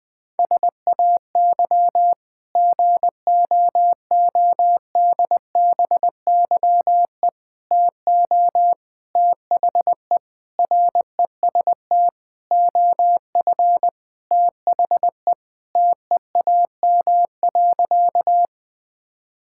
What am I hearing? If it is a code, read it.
Morse code, heard as a keyed tone, about 20 wpm, 700 Hz. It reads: SAY GOODBYE TO THE REST OF THE TEAM.
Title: Sentences with 5-Word Phrases - 20wpm